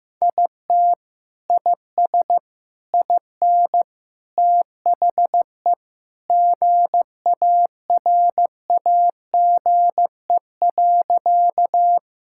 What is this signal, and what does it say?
Morse code, heard as a keyed tone, about 14 wpm, 700 Hz. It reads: IT IS IN THE GARAGE.
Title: Sentences from Top 1600 Words 15wpm